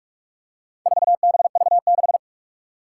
Morse code, heard as a keyed tone, about 44 wpm, 700 Hz. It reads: 4BV6
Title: Four Challenging Letter-Number Combinations 45wpm